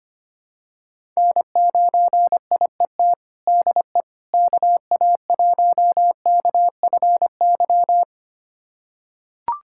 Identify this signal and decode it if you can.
Morse code, heard as a keyed tone, about 25 wpm, 700 Hz. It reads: N9IET DE KA1KFY
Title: USA Callsign DE Callsign 25wpm